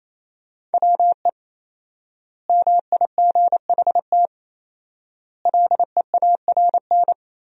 Morse code, heard as a keyed tone, about 28 wpm, 700 Hz. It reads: WE MIGHT LEARN
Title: 4x Word-Spacing - Sentences from Top 300 Words - 28wpm